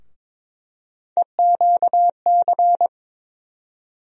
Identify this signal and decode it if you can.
Morse code, heard as a keyed tone, about 21 wpm, 700 Hz. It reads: EQC
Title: USA Call Sign Postfixes - 3 Letters 22wpm